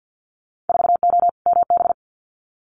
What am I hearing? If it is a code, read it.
Morse code, heard as a keyed tone, about 50 wpm, 700 Hz. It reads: 4O MB
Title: Winter Field Day Exchanges Encore 50wpm